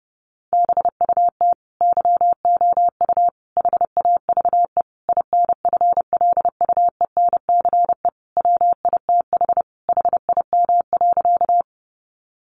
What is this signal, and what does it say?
Morse code, heard as a keyed tone, about 30 wpm, 700 Hz. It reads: BUT YOU HAVE INFLUENCE WITH HIM.
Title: Sentences from Top 1100 Words 30wpm